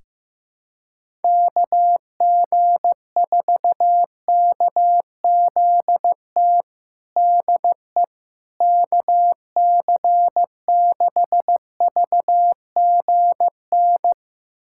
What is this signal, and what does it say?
Morse code, heard as a keyed tone, about 15 wpm, 700 Hz. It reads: KG4KZT DE KC6VGN
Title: USA Callsign DE Callsign 15wpm